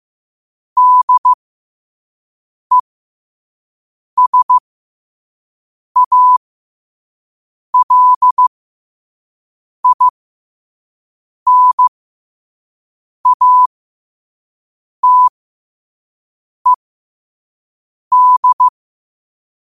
Morse code, heard as a keyed tone, about 15 wpm, spaced as for 5 wpm, 1000 Hz. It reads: DESALINATED
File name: 5wpm